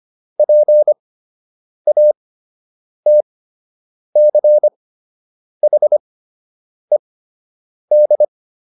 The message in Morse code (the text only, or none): PATCHED